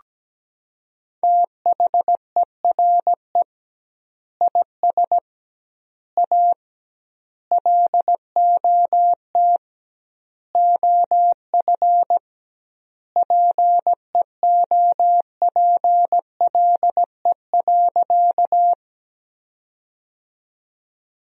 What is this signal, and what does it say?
Morse code, heard as a keyed tone, about 17 wpm, 700 Hz. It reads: THERE IS A LOT OF PEOPLE.
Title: 2x Word-Spacing - Sentences from Top 500 Words - 17wpm